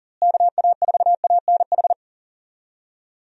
Morse code, heard as a keyed tone, about 40 wpm, 700 Hz. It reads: KA4ANH
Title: USA Call Signs - 6 Characters - Repeated 2x - 40wpm